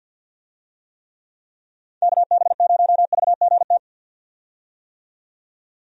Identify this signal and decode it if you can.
Morse code, heard as a keyed tone, about 51 wpm, 700 Hz. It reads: KB0VGT
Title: USA Call Signs - 6 Characters - 50wpm